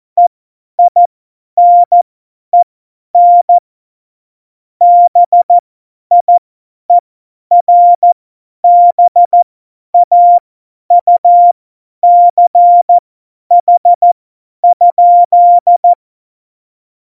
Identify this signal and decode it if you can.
Morse code, heard as a keyed tone, about 14 wpm, spaced as for 10 wpm, 700 Hz. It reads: EINEN BIERBAUCH?
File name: Morsekurs / Witze / Witz 15. Oktober 2019, 50 F70